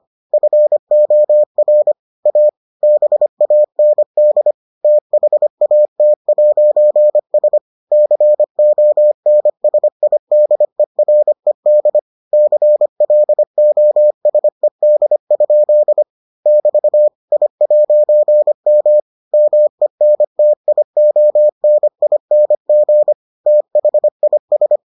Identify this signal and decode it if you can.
Morse code, heard as a keyed tone, about 25 wpm, 600 Hz. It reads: FOR A BAND THAT'S CONSIDERED CLOSED? <BT> I'M MENTIONING THIS